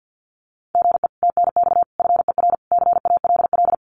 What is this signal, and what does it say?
Morse code, heard as a keyed tone, about 50 wpm, 700 Hz. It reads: GE TNX FER CALL
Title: Common QSO Phrases Encore 50wpm